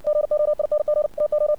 Morse code, heard as a keyed tone, about 60 wpm, 600 Hz. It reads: GOING TO
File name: archived / CRSnet